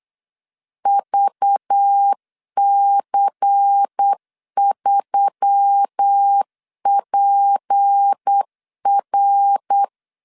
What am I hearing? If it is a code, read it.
Morse code, heard as a keyed tone, about 8 wpm, 800 Hz. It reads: VC3PR
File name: 8wpm